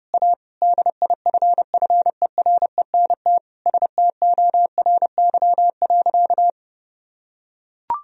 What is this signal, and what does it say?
Morse code, heard as a keyed tone, about 30 wpm, 700 Hz. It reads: A DIFFERENT STORY.
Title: Sentences from Top 1000 Words - Review Part 3 - 30wpm